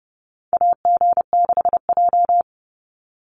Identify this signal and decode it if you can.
Morse code, heard as a keyed tone, about 30 wpm, 700 Hz. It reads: AG6J
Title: USA Call Signs 30wpm